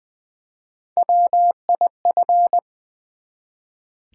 Morse code, heard as a keyed tone, about 20 wpm, 700 Hz. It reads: WIF
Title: Build A Word - 1 Letter At A Time - 3 to 5-Letter Words - 20wpm